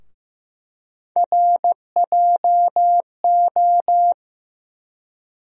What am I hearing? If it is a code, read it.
Morse code, heard as a keyed tone, about 15 wpm, 700 Hz. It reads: RJO
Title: USA Call Sign Postfixes - 3 Letters 15wpm